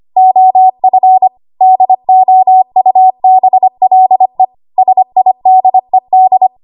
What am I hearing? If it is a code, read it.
Morse code, heard as a keyed tone, about 25 wpm, 750 Hz. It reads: OF DOUBLE SIDED
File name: Archive / 25 WPM / 25WPM